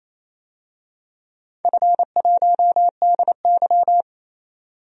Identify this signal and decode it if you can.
Morse code, heard as a keyed tone, about 29 wpm, 700 Hz. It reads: F1DY
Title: DXCC-100 Call Signs - 4 Characters - 28wpm